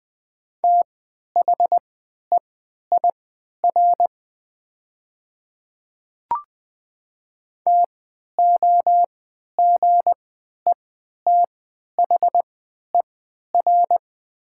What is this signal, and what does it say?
Morse code, heard as a keyed tone, about 20 wpm, spaced as for 11 wpm, 700 Hz. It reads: THEIR TOGETHER
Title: 3x Character Spacing - Top 500 Words - 20wpm